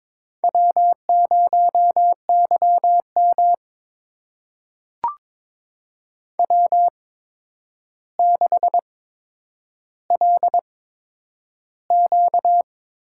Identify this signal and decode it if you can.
Morse code, heard as a keyed tone, about 22 wpm, spaced as for 9 wpm, 700 Hz. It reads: W0YM W6LQ
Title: USA Call Signs - 4 Characters - 8x Character Spacing - 22wpm